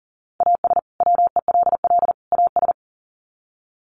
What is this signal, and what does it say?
Morse code, heard as a keyed tone, about 40 wpm, 700 Hz. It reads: AS WELL AS
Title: Sets of 3 Words 40wpm